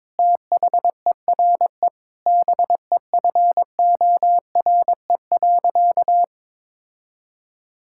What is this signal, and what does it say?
Morse code, heard as a keyed tone, about 21 wpm, 700 Hz. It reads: THERE BEFORE.
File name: practice / 22wpm / Sentences with 3-Word Phrases - 22wpm